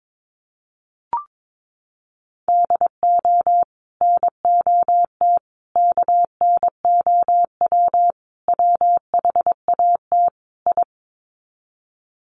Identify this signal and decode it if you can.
Morse code, heard as a keyed tone, about 22 wpm, 700 Hz. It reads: DO NOT KNOW WHAT I
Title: Sets of 5 Words 22wpm